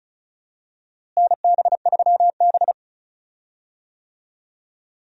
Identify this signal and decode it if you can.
Morse code, heard as a keyed tone, about 36 wpm, 700 Hz. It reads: NB3B
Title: USA Call Signs - 4 Characters - Repeated 3x - 35wpm